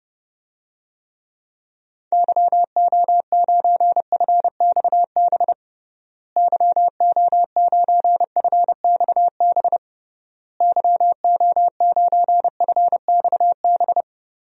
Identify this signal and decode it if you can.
Morse code, heard as a keyed tone, about 32 wpm, 700 Hz. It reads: YO9FXB YO9FXB YO9FXB
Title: DXCC-100 Call Signs - 6 Characters - Repeated 3x - 30wpm